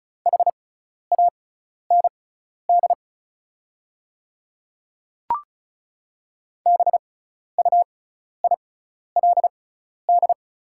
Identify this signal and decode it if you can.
Morse code, heard as a keyed tone, about 36 wpm, spaced as for 12 wpm, 690 Hz. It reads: HAND BUILD
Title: 6x Character Spacing - Top 200 Words - 35wpm